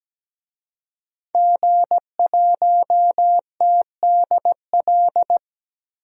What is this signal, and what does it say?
Morse code, heard as a keyed tone, about 17 wpm, 700 Hz. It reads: G1TDL
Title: DXCC-100 Call Signs - 5 Characters - 17wpm